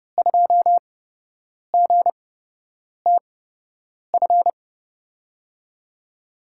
Morse code, heard as a keyed tone, about 32 wpm, spaced as for 8 wpm, 700 Hz. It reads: 2GTF
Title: USA Call Signs - 5 Characters - 8x Character Spacing - 30wpm